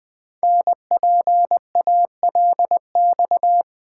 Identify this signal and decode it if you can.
Morse code, heard as a keyed tone, about 20 wpm, 700 Hz. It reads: NPALX
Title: Code Groups - 5 Letter:Numbers - 20wpm